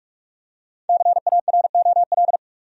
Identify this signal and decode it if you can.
Morse code, heard as a keyed tone, about 46 wpm, 700 Hz. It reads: CAROL
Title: USA Common Names - 45wpm